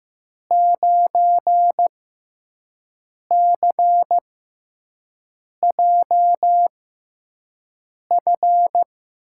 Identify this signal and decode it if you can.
Morse code, heard as a keyed tone, about 15 wpm, spaced as for 5 wpm, 700 Hz. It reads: 9CJF
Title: USA Call Signs - 5 Characters - 6x Character Spacing - 15wpm